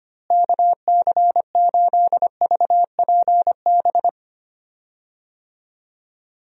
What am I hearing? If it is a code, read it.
Morse code, heard as a keyed tone, about 25 wpm, 700 Hz. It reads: KC8VPB
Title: USA Call Signs - 6 Characters - Repeated 2x - 25wpm